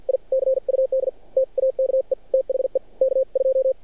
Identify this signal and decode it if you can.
Morse code, heard as a keyed tone, about 50 wpm, spaced as for 43 wpm, 530 Hz. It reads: I CUD TAKE THE K2
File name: archived / CRSnet